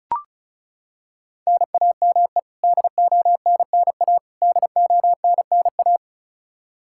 Morse code, heard as a keyed tone, about 34 wpm, 700 Hz. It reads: NAME DONNA DONNA
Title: K1USN SST QSO Practice 35wpm